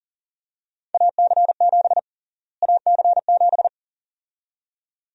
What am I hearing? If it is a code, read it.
Morse code, heard as a keyed tone, about 40 wpm, 700 Hz. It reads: AC7 AC7
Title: USA Call Sign Prefixes - Repeated 2x - 40wpm